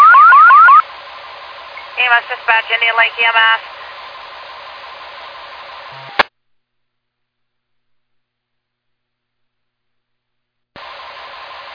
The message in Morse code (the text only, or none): none